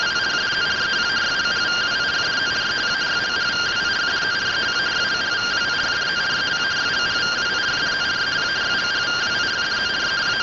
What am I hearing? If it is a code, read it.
Two-tone teleprinter signal, about 170 Hz shift.
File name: findings / AXIAL fr130 3mm 409.gen-19 rtty rx